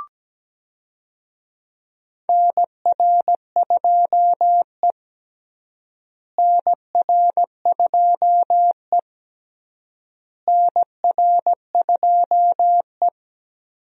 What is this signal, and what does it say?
Morse code, heard as a keyed tone, about 17 wpm, 700 Hz. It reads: NR2E NR2E NR2E
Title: USA Call Signs - 4 Characters - Repeated 3x - 17wpm